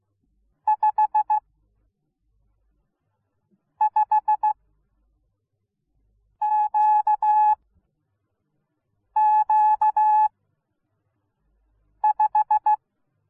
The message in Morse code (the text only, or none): 55QQ5